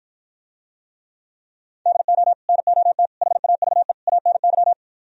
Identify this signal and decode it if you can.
Morse code, heard as a keyed tone, about 53 wpm, 700 Hz. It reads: DO NOT HAVE ANY